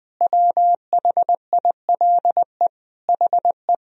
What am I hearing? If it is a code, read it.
Morse code, heard as a keyed tone, about 20 wpm, 700 Hz. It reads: WHILE HE
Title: Sentences from Top 1000 Words - Review Part 1 - 20wpm